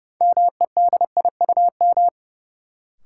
Morse code, heard as a keyed tone, about 30 wpm, 700 Hz. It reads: MEDIUM
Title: Sentences from Top 900 Words - New Word Drill - 30wpm